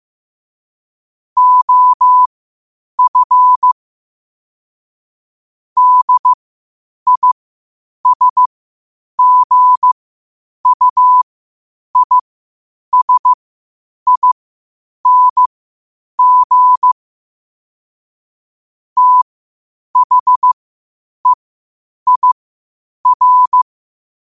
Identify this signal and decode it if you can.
Morse code, heard as a keyed tone, about 15 wpm, spaced as for 8 wpm, 1000 Hz. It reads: OF DISGUISING THEIR
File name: 8wpm